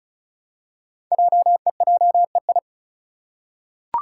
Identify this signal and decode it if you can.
Morse code, heard as a keyed tone, about 36 wpm, 700 Hz. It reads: JEJEI
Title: Code Groups - 5 Letter:Numbers - 35wpm